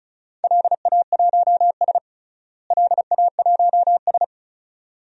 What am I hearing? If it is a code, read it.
Morse code, heard as a keyed tone, about 34 wpm, 700 Hz. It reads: LA1S LA1S
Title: DXCC-100 Call Signs - 4 Characters - Repeated 2x - 35wpm